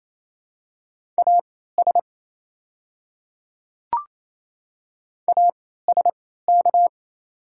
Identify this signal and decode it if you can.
Morse code, heard as a keyed tone, about 28 wpm, spaced as for 16 wpm, 700 Hz. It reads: AS ASK
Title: 3x Character Spacing - Top 300 Words - 28wpm